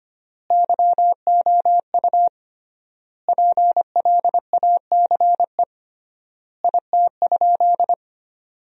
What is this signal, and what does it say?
Morse code, heard as a keyed tone, about 25 wpm, 700 Hz. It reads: YOU PLACE IT?
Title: 3x Word-Spacing - Sentences from Top 200 Words - 25wpm